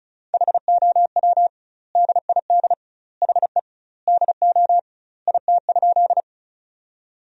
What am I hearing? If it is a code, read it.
Morse code, heard as a keyed tone, about 36 wpm, 700 Hz. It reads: HOW DID HE DO IT?
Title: 2x Word-Spacing - Sentences from Top 100 Words - 35wpm